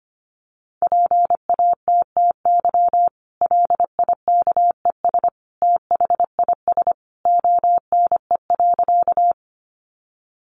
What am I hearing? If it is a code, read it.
Morse code, heard as a keyed tone, about 25 wpm, 700 Hz. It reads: PATTY LIKES THIS ONE.
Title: Additional Easy Phrases 25wpm